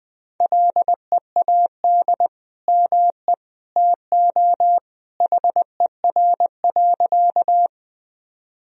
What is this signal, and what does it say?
Morse code, heard as a keyed tone, about 20 wpm, 700 Hz. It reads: LEAD ME TO HER.
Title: Sentences from Top 1000 Words - Review Part 2 - 20wpm